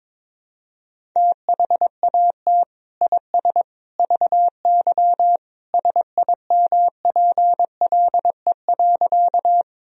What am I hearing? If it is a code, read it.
Morse code, heard as a keyed tone, about 21 wpm, 700 Hz. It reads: THAT IS VY SIMPLE.
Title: Sentences from Top 400 Words and Abbreviations 22wpm